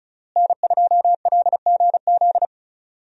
Morse code, heard as a keyed tone, about 36 wpm, 700 Hz. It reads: N2LGZ
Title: USA Call Signs - 5 Characters - Repeated 3x - 35wpm